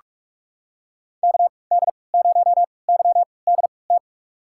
Morse code, heard as a keyed tone, about 46 wpm, spaced as for 26 wpm, 700 Hz. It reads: KD0YDT
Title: USA Call Signs - 6 Characters - 3x Character Spacing - 45wpm